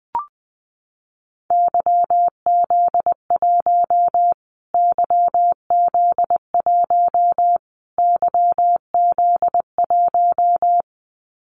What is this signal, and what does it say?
Morse code, heard as a keyed tone, about 20 wpm, 700 Hz. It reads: YZ1 YZ1 YZ1
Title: Callsign Prefixes for Top 25 DXCC - Repeated 3x - 20wpm